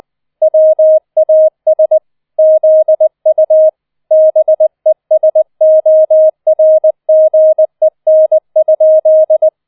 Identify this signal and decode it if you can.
Morse code, heard as a keyed tone, about 20 wpm, 630 Hz. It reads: WAS ZU BESORGEN?